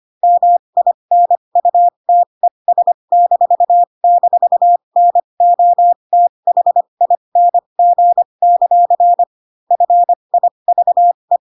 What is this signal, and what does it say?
Morse code, heard as a keyed tone, about 25 wpm, spaced as for 22 wpm, 700 Hz. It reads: MINUTES--NOTHING; FIVE